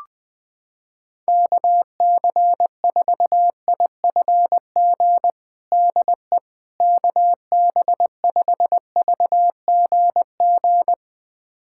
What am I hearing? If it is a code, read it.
Morse code, heard as a keyed tone, about 20 wpm, 700 Hz. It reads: KC4IFG DE KB5VGG
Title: USA Callsign DE Callsign - 6 Characters - 20wpm